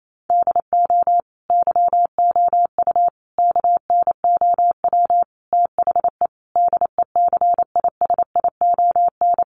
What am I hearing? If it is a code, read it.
Morse code, heard as a keyed tone, about 28 wpm, 700 Hz. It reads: DO YOU KNOW THE DECISION
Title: Sentences from Top 1000 Words 28wpm